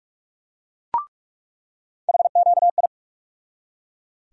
Morse code, heard as a keyed tone, about 42 wpm, 700 Hz. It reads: HQI
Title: USA Call Sign Postfixes - 3 Letters 45wpm